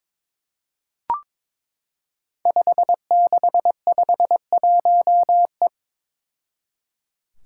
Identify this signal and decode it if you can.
Morse code, heard as a keyed tone, about 22 wpm, 700 Hz. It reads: 5651E
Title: Five Challenging Letter-Number Combinations 22wpm